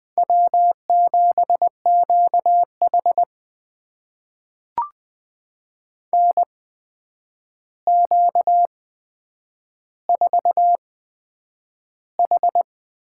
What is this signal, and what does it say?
Morse code, heard as a keyed tone, about 20 wpm, spaced as for 9 wpm, 700 Hz. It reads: W7QH NQ4H
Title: USA Call Signs - 8x Character Spacing - 20wpm